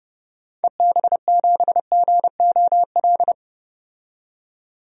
Morse code, heard as a keyed tone, about 30 wpm, 700 Hz. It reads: EB7GOL
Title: DXCC-100 Call Signs - 6 Characters - 30wpm